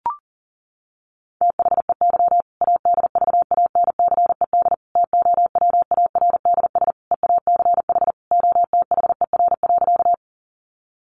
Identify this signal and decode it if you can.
Morse code, heard as a keyed tone, about 40 wpm, 700 Hz. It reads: THEY ADVANCED TOWARDS EACH OTHER.
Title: Sentences from Top 1200 Words 40wpm